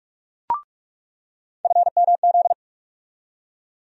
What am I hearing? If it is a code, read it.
Morse code, heard as a keyed tone, about 47 wpm, 700 Hz. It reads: FMZ
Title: USA Call Sign Postfixes - 3 Letters 45wpm